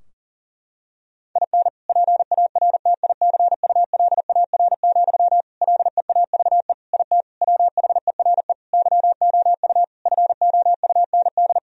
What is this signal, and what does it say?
Morse code, heard as a keyed tone, about 40 wpm, 700 Hz. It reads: IN PARTICULAR, LEAVE IT WHERE YOU FOUND
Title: Sentences from Top 1000 Words - Review Part 4 - 40wpm